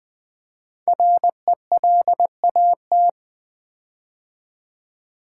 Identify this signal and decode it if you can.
Morse code, heard as a keyed tone, about 20 wpm, 700 Hz. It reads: RELAT
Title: Build A Word - 5 Letters At A Time - 10-Letter Words - 20wpm